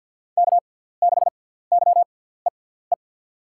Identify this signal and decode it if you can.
Morse code, heard as a keyed tone, about 47 wpm, spaced as for 17 wpm, 700 Hz. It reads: K6YEE
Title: USA Call Signs - 5 Characters - 6x Character Spacing - 50wpm